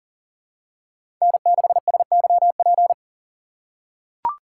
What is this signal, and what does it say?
Morse code, heard as a keyed tone, about 40 wpm, 700 Hz. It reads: N6SYP
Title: USA Call Signs - 5 Characters - 3x Character Spacing - 40wpm